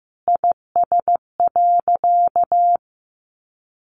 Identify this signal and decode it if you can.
Morse code, heard as a keyed tone, about 15 wpm, 700 Hz. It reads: IS.
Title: Sentences from Top 1100 Words 15wpm